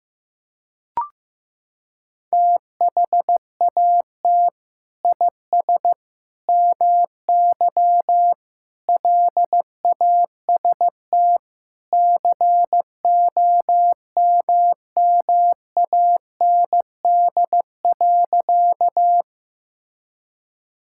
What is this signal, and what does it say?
Morse code, heard as a keyed tone, about 15 wpm, 700 Hz. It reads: THAT IS MY LAST COMMAND.
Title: Sentences from Top 1600 Words 15wpm